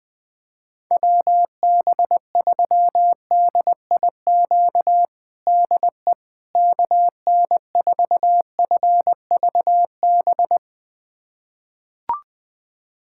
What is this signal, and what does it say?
Morse code, heard as a keyed tone, about 20 wpm, 700 Hz. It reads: WB3DIQ DE KN4FVB
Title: USA Callsign DE Callsign - 6 Characters - 20wpm